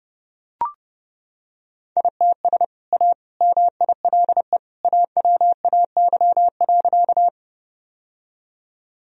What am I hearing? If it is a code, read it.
Morse code, heard as a keyed tone, about 30 wpm, 700 Hz. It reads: ITS A MILE AWAY.
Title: Sentences from Top 1000 Words - Review Part 1 - 30wpm